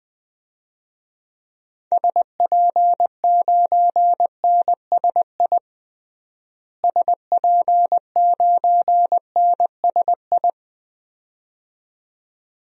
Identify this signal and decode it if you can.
Morse code, heard as a keyed tone, about 20 wpm, 700 Hz. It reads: SP9NSI SP9NSI
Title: DXCC-100 Call Signs - 6 Characters - Repeated 2x - 20wpm